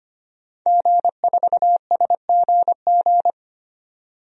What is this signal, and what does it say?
Morse code, heard as a keyed tone, about 25 wpm, 700 Hz. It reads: G4SGG
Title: DXCC-100 Call Signs - 5 Characters - 25wpm